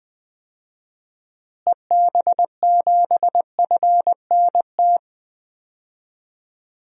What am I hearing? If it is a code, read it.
Morse code, heard as a keyed tone, about 20 wpm, 700 Hz. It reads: EB7FNT
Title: DXCC-100 Call Signs - 6 Characters - Repeated 2x - 20wpm